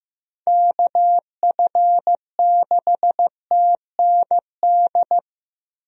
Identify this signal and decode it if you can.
Morse code, heard as a keyed tone, about 15 wpm, 700 Hz. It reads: KF6TND
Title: USA Call Signs - 6 Characters - 15wpm